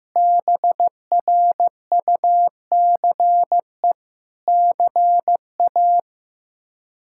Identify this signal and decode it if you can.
Morse code, heard as a keyed tone, about 15 wpm, 700 Hz. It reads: BRUCE CA
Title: K1USN SST QSO Practice - 15wpm